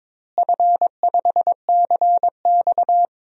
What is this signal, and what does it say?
Morse code, heard as a keyed tone, about 22 wpm, 700 Hz. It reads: F5CX
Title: DXCC-100 Call Signs - 4 Characters - Repeated 2x - 22wpm